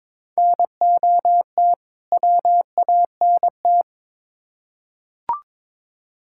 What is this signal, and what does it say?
Morse code, heard as a keyed tone, about 21 wpm, 700 Hz. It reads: NOT WANT
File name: Build A Phrase - 2-Word Phrases - 22wpm